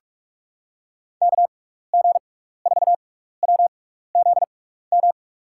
Morse code, heard as a keyed tone, about 46 wpm, 700 Hz. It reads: K G 4 W Z M
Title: USA Call Signs - 6 Characters - 6x Character Spacing - 45wpm